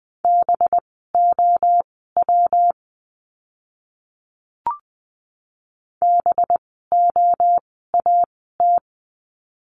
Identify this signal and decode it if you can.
Morse code, heard as a keyed tone, about 20 wpm, spaced as for 15 wpm, 700 Hz. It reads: BOW BOAT
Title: 185 Lesson 30f - Letter b - Words 20wpm